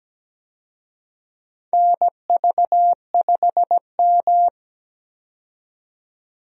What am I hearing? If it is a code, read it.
Morse code, heard as a keyed tone, about 17 wpm, 700 Hz. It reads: NV5M
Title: USA Call Signs - 4 Characters - 17wpm